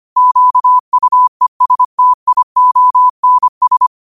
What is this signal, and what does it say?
Morse code, heard as a keyed tone, about 25 wpm, 1000 Hz. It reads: QUESTIONS
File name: qotdfiles / 25wpm